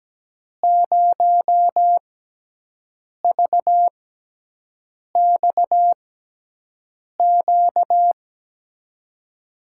Morse code, heard as a keyed tone, about 17 wpm, spaced as for 6 wpm, 700 Hz. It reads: 0VXQ
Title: USA Call Signs - 6x Character Spacing - 17wpm